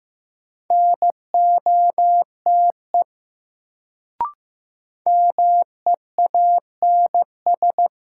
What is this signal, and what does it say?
Morse code, heard as a keyed tone, about 15 wpm, 700 Hz. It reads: NOTE MEANS
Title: Sets of 5 Words - New Word Drill - 15wpm